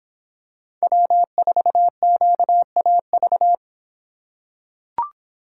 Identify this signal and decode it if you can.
Morse code, heard as a keyed tone, about 25 wpm, 700 Hz. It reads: W4QAV